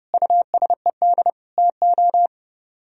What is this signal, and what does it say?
Morse code, heard as a keyed tone, about 35 wpm, 700 Hz. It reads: USED TO
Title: Speed-Racing - Sets of 4 Words 30wpm